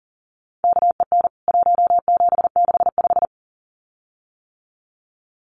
Morse code, heard as a keyed tone, about 40 wpm, 700 Hz. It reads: KEN 1765
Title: CWT Exchange 40wpm